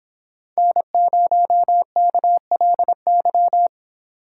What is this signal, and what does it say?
Morse code, heard as a keyed tone, about 24 wpm, 700 Hz. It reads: N0KLY
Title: Speed Racing - USA 1x3 Call Signs 22wpm